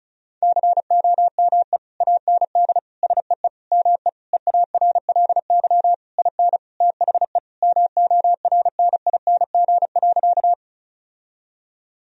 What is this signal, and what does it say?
Morse code, heard as a keyed tone, about 34 wpm, 700 Hz. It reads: COME AND SEE ME EARLY IN THE MORNING.
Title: Sentences from Top 1000 Words - Review Part 2 - 35wpm